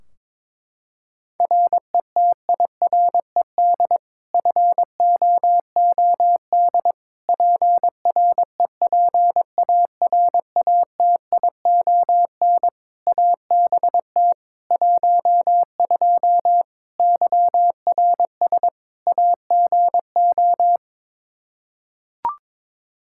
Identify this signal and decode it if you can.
Morse code, heard as a keyed tone, about 22 wpm, 700 Hz. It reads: RETIRED FOOD PREPARATION ABT 12 YRS AGO
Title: K1USN SST QSO Practice 22wpm